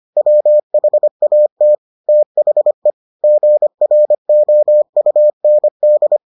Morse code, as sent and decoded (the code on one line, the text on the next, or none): .-- .... .- - / - .... / --. .-. --- ..- -. -..
WHAT THE GROUND